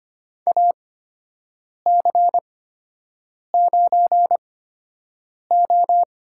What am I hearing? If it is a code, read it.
Morse code, heard as a keyed tone, about 25 wpm, spaced as for 7 wpm, 700 Hz. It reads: AC9O
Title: USA Call Signs - 8x Character Spacing - 25wpm